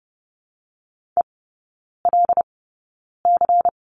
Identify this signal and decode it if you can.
Morse code, heard as a keyed tone, about 30 wpm, spaced as for 9 wpm, 700 Hz. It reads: ELC
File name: Lesson 16e - Letter c - Three Characters 30wpm